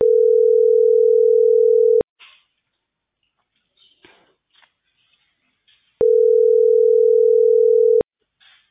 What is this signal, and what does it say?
Ringback tone (North America), 440 + 480 Hz.